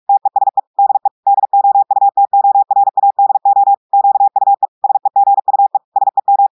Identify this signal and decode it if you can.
Morse code, heard as a keyed tone, about 46 wpm, 800 Hz. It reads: TESE DE DOUTORADO QUE SEGUE SEM
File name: audios / portugais / portugais texte en clair 1 vitesse 45 mots minutes